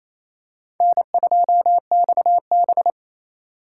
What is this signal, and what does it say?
Morse code, heard as a keyed tone, about 28 wpm, 700 Hz. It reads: N2XB
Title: USA Call Signs - 4 Characters - 4x Character Spacing - 28wpm